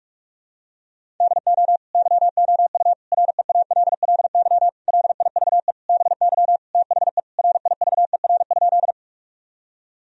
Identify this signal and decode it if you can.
Morse code, heard as a keyed tone, about 46 wpm, 700 Hz. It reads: DO YOU REALLY LIVE BY THE RIVER?
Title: Sentences from Top 1000 Words - Review Part 1 - 45wpm